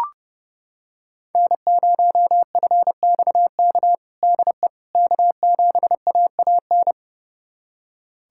Morse code, heard as a keyed tone, about 30 wpm, 700 Hz. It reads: N0FXK DE K7AAN
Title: USA Callsign DE Callsign - 5 Characters - 30wpm